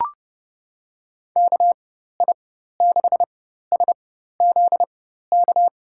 Morse code, heard as a keyed tone, about 30 wpm, spaced as for 14 wpm, 700 Hz. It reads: KI6SZK
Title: USA Call Signs - 6 Characters - 4x Character Spacing - 30wpm